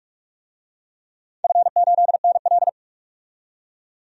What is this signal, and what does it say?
Morse code, heard as a keyed tone, about 46 wpm, 700 Hz. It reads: F8NL
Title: DXCC-100 Call Signs - 4 Characters - 45wpm